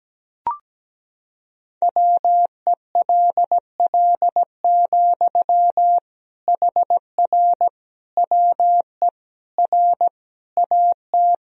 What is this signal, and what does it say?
Morse code, heard as a keyed tone, about 17 wpm, 700 Hz. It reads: WELL, HR WE R AT
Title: Sentences from Top 300 Words and Abbreviations 17wpm